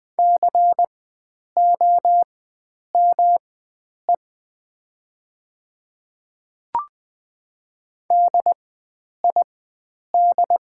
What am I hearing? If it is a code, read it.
Morse code, heard as a keyed tone, about 20 wpm, spaced as for 9 wpm, 700 Hz. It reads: COME DID
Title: 4x Character Spacing - Top 100 Words - 20wpm